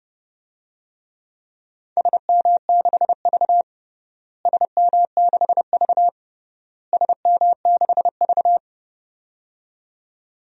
Morse code, heard as a keyed tone, about 30 wpm, 700 Hz. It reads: SM6V SM6V SM6V
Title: DXCC-100 Call Signs - 4 Characters - Repeated 3x - 30wpm